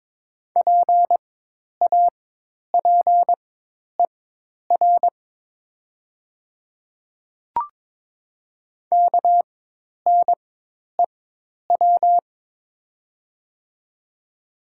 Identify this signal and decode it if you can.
Morse code, heard as a keyed tone, about 23 wpm, spaced as for 10 wpm, 700 Hz. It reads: PAPER KNEW
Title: 4x Character Spacing - Top 500 Words - 22wpm